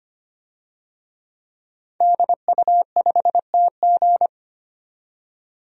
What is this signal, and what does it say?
Morse code, heard as a keyed tone, about 25 wpm, 700 Hz. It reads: DU5TG
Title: DXCC-100 Call Signs - 5 Characters - 25wpm